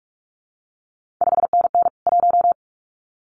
Morse code, heard as a keyed tone, about 46 wpm, 700 Hz. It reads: 5NN 1
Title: CQ WW DX Contest Exchange Encore - 45wpm